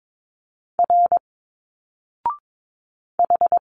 Morse code, heard as a keyed tone, about 22 wpm, 700 Hz. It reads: R H
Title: Single Letters 22wpm